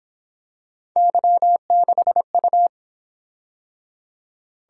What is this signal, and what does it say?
Morse code, heard as a keyed tone, about 28 wpm, 700 Hz. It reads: Y6U